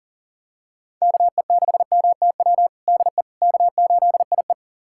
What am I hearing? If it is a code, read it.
Morse code, heard as a keyed tone, about 40 wpm, 700 Hz. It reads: KE6MTW DE K8IE
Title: USA Callsign DE Callsign 40wpm